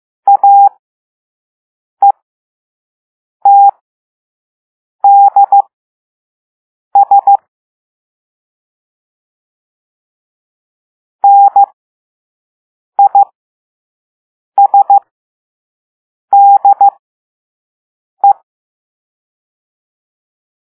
Morse code, heard as a keyed tone, about 15 wpm, spaced as for 5 wpm, 800 Hz. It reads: AETDS NISDE